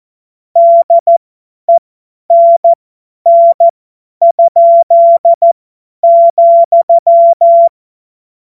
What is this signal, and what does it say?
Morse code, heard as a keyed tone, about 14 wpm, spaced as for 10 wpm, 680 Hz. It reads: DENN?,
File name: Morsekurs / Witze / Witz 04. April 2023 50 F70